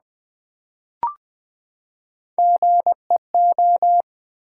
Morse code, heard as a keyed tone, about 20 wpm, 700 Hz. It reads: GEO
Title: Build A Word - 3 Letters At A Time - 9-Letter Words - 20wpm